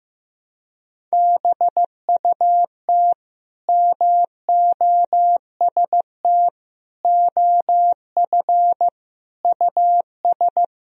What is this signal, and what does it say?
Morse code, heard as a keyed tone, about 15 wpm, 700 Hz. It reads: BUT MOST OF US
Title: Sentences from Top 1000 Words - Review Part 1 - 15wpm